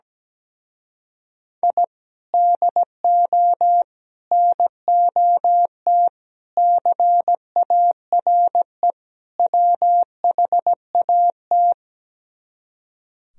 Morse code, heard as a keyed tone, about 17 wpm, 700 Hz. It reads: I DO NOT CARE WHAT
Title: Speed-Racing - Sets of 5 Words 17wpm